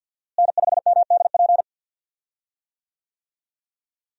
Morse code, heard as a keyed tone, about 47 wpm, 700 Hz. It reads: N5MDP
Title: USA Call Signs - 5 Characters - Repeated 3x - 50wpm